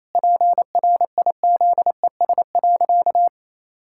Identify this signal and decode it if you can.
Morse code, heard as a keyed tone, about 28 wpm, 700 Hz. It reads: PRIZES.
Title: Pangram Challenge v2 - 28wpm